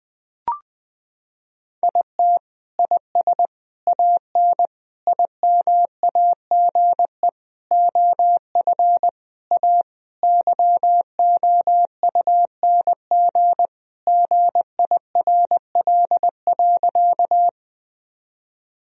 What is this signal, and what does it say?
Morse code, heard as a keyed tone, about 20 wpm, 700 Hz. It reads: IT IS AN IMAGE OF A YOUNG GIRL.
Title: Sentences from Top 1000 Words - Review Part 4 - 20wpm